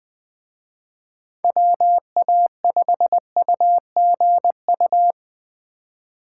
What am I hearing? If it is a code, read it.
Morse code, heard as a keyed tone, about 20 wpm, 700 Hz. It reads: WA5UGU
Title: Speed Racing - USA 2x3 Call Signs 17wpm